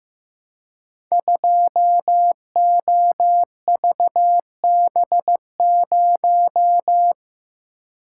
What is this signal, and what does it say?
Morse code, heard as a keyed tone, about 15 wpm, 700 Hz. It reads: 2OVB0
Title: Code Groups - 5 Letter:Numbers - 15wpm